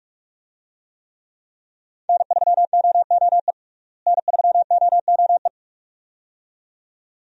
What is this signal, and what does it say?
Morse code, heard as a keyed tone, about 47 wpm, 700 Hz. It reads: N3OOE N3OOE
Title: USA Call Signs - 5 Characters - Repeated 2x - 45wpm